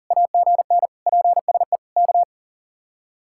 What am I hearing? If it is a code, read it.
Morse code, heard as a keyed tone, about 53 wpm, 700 Hz. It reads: AGN PSE K